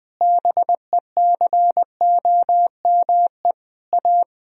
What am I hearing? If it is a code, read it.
Morse code, heard as a keyed tone, about 21 wpm, 700 Hz. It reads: BECOME A
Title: Sentences from Top 1000 Words - Review Part 4 - 20wpm